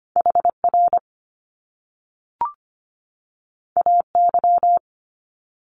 Morse code, heard as a keyed tone, about 25 wpm, 700 Hz. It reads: HR AY
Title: Top Two Letter Combinations in Words 25wpm